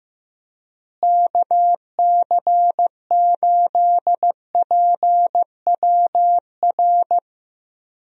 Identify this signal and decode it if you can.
Morse code, heard as a keyed tone, about 15 wpm, 700 Hz. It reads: KC8PWR
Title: USA Call Signs - 6 Characters - Repeated 3x - 15wpm